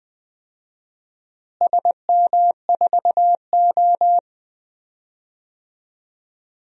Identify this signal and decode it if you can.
Morse code, heard as a keyed tone, about 20 wpm, 700 Hz. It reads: SM4O
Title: DXCC-100 Call Signs - 4 Characters - Repeated 3x - 20wpm